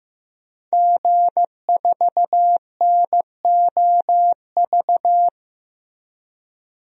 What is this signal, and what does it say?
Morse code, heard as a keyed tone, about 15 wpm, 700 Hz. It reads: G4NOV
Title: DXCC-100 Call Signs - 5 Characters - Repeated 2x - 15wpm